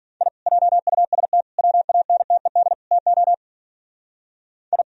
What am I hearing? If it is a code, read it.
Morse code, heard as a keyed tone, about 43 wpm, 700 Hz. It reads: I JUST WANTED TO I
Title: Speed-Racing - Sets of 4 Words 40wpm